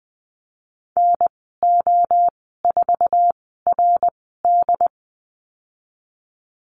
Morse code, heard as a keyed tone, about 20 wpm, spaced as for 14 wpm, 700 Hz. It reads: NO4RD
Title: 076 Lesson 13g - Letter L - Call Signs 20wpm